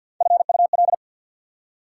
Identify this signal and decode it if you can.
Morse code, heard as a keyed tone, about 48 wpm, 690 Hz. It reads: FUL
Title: Prefixes and Suffixes 50wpm